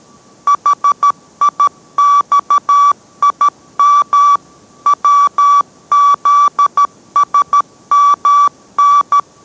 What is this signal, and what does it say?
Morse code, heard as a keyed tone, about 13 wpm, 1150 Hz. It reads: HIXIM WZSMN